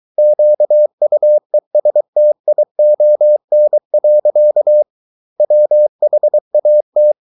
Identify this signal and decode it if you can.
Morse code, heard as a keyed tone, about 23 wpm, 600 Hz. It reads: QUESTION. WHAT